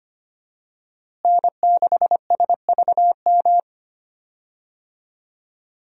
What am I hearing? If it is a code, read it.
Morse code, heard as a keyed tone, about 25 wpm, 700 Hz. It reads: N6SVM
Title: USA Call Signs - 5 Characters - Repeated 3x - 25wpm